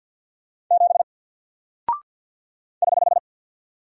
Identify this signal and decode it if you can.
Morse code, heard as a keyed tone, about 48 wpm, 700 Hz. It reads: 7 <HH>
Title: Single Character (Letter:Number:Punctuation:Prosign) Alt 50wpm